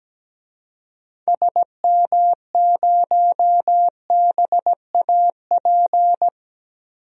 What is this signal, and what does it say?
Morse code, heard as a keyed tone, about 17 wpm, 700 Hz. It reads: SM0BAP
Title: DXCC-100 Call Signs - 6 Characters - 17wpm